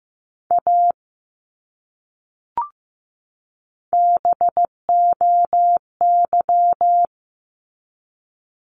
Morse code, heard as a keyed tone, about 15 wpm, 700 Hz. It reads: A BOY
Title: Top 200 Words 15wpm